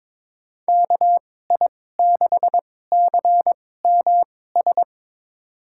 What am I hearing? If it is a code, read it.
Morse code, heard as a keyed tone, about 22 wpm, spaced as for 16 wpm, 700 Hz. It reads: KI6CMS
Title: USA Call Signs - 6 Characters - 2x Character Spacing - 22wpm